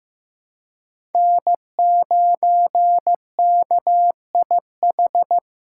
Morse code, heard as a keyed tone, about 15 wpm, 700 Hz. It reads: N9KIH
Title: USA Call Signs - 4x Character Spacing - 15wpm